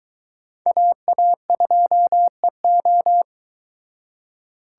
Morse code, heard as a keyed tone, about 23 wpm, 700 Hz. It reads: AA2EO